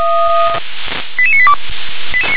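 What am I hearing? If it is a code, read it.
Five-tone selective call (ZVEI), heard as quick runs of five tones, tones of about 70 ms. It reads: 88982
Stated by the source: ZVEI five-tone selective call 88981 received by SDR.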